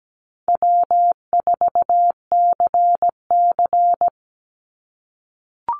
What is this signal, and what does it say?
Morse code, heard as a keyed tone, about 17 wpm, 700 Hz. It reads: W4CC
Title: Call Signs - 17wpm